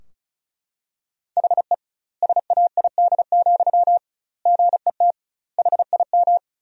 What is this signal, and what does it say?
Morse code, heard as a keyed tone, about 36 wpm, 700 Hz. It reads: HE SAID, GET HIM
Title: 2x Word-Spacing - Sentences from Top 400 Words - 35wpm